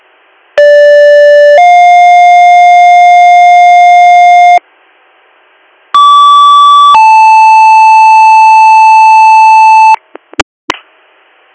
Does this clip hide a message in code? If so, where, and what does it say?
Yes: two-tone page from 0.6 to 10.0 s, heard as a one-second tone, then a three-second tone, about 580 then 710 Hz, about 1150 then 880 Hz.